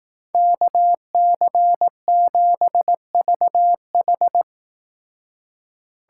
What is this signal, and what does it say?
Morse code, heard as a keyed tone, about 18 wpm, 700 Hz. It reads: KC7VH
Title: Speed Racing - USA 2x2 Call Signs 15wpm